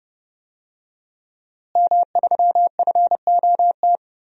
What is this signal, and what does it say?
Morse code, heard as a keyed tone, about 30 wpm, 700 Hz. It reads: M3FOT
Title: DXCC-100 Call Signs - 5 Characters - Repeated 3x - 30wpm